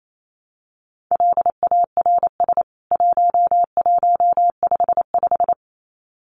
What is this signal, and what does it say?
Morse code, heard as a keyed tone, about 28 wpm, 700 Hz. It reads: LARS 1155
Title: CWT Exchange 28wpm